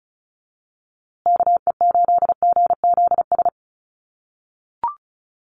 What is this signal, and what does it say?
Morse code, heard as a keyed tone, about 34 wpm, 700 Hz. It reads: KE8GZS
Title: USA Call Signs Repeated 3x 35wpm